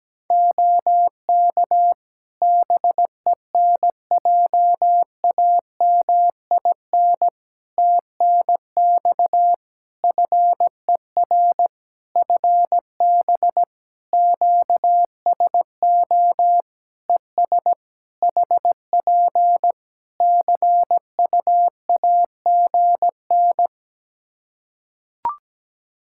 Morse code, heard as a keyed tone, about 17 wpm, 700 Hz. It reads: OK BENJAMIN TNX FER FB QSO ES HP CUAGN
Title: Practice QSO - Exchange 4 - Ending - 17wpm